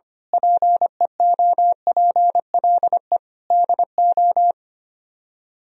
Morse code, heard as a keyed tone, about 24 wpm, 700 Hz. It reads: PEOPLE DO
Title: Speed-Racing - Sets of 5 Words 25wpm